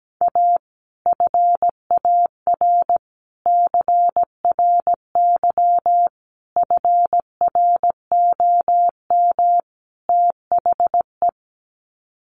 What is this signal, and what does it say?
Morse code, heard as a keyed tone, about 17 wpm, 700 Hz. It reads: A FAR CRY FROM THE
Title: Sets of 5 Words Encore 17wpm